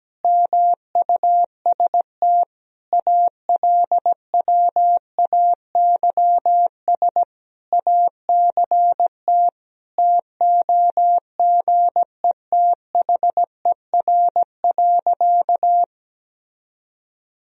Morse code, heard as a keyed tone, about 17 wpm, 700 Hz. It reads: MUST ALWAYS ACT TOGETHER.
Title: Sentences from Top 1000 Words - Review Part 1 - 17wpm